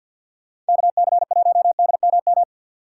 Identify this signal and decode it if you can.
Morse code, heard as a keyed tone, about 49 wpm, 700 Hz. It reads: KC1DMM
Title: USA Call Signs - 6 Characters - 3x Character Spacing - 50wpm